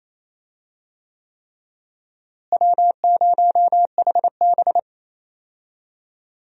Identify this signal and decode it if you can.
Morse code, heard as a keyed tone, about 27 wpm, 700 Hz. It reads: W0HB
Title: USA Call Signs - 4 Characters - Repeated 3x - 28wpm